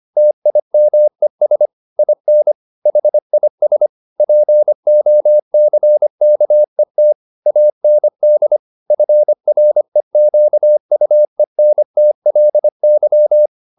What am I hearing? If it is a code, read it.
Morse code, heard as a keyed tone, about 25 wpm, 600 Hz. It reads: TIMES IN HIS POCKET AND FREQUENTLY
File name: the old man and the sea part 1 25wpm